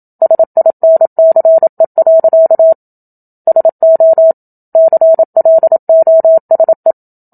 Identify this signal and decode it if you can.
Morse code, heard as a keyed tone, about 27 wpm, 650 Hz. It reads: SINCE. SO CLOSE
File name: Book Two / 01-Chapter 1 - 27 WPM